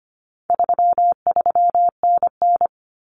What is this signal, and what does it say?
Morse code, heard as a keyed tone, about 25 wpm, 700 Hz. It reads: 33NN
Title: Common Cut Numbers - 4-Digit 25wpm